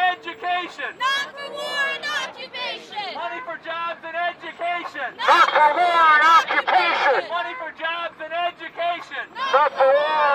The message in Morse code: none